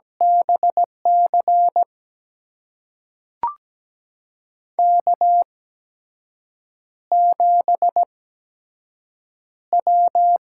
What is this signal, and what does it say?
Morse code, heard as a keyed tone, about 17 wpm, spaced as for 6 wpm, 700 Hz. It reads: BC K7W
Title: USA Call Signs - 8x Character Spacing - 17wpm